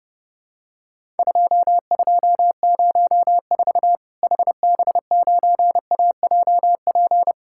Morse code, heard as a keyed tone, about 30 wpm, 700 Hz. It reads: 2204 HB9AJP
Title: Worked All Europe DX Contest - QTC Exchange - 30wpm